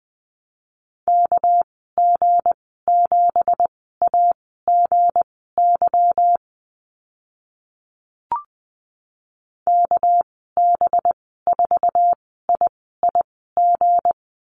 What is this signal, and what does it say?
Morse code, heard as a keyed tone, about 20 wpm, spaced as for 14 wpm, 700 Hz. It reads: KG7AGY KB4IIG
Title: 219 Lesson 35g - Number 0 - Call Signs 20wpm